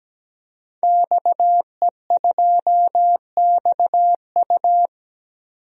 Morse code, heard as a keyed tone, about 17 wpm, 700 Hz. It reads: XE2XU
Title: DXCC-100 Call Signs - 5 Characters - 17wpm